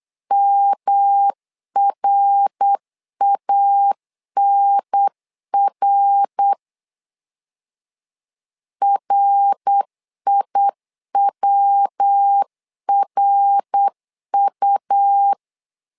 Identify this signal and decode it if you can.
Morse code, heard as a keyed tone, about 8 wpm, 800 Hz. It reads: MRANR RIWRU
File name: LEZIONI / 3 MWRO / 3 MWR 8wpm 2444